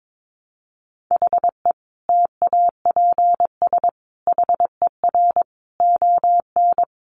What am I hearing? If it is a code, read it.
Morse code, heard as a keyed tone, about 22 wpm, 700 Hz. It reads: HE TAPS HER ON
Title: Additional Easy Phrases 22wpm